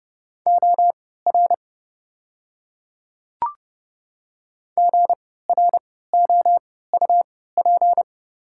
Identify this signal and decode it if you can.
Morse code, heard as a keyed tone, about 30 wpm, spaced as for 17 wpm, 690 Hz. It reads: OR GROUP
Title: 3x Character Spacing - Top 500 Words - 30wpm